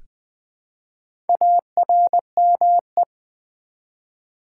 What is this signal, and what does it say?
Morse code, heard as a keyed tone, about 20 wpm, 700 Hz. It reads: ARME
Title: Build A Word - 1 Letter At A Time - 5-Letter Words - 20wpm